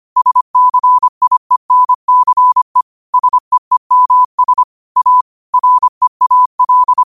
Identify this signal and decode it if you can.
Morse code, heard as a keyed tone, about 25 wpm, 1000 Hz. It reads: SCIENCE SEEMS A REAL